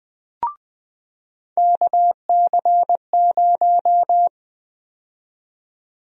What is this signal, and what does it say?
Morse code, heard as a keyed tone, about 20 wpm, 700 Hz. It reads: KC0
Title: USA Call Sign Prefixes - 20wpm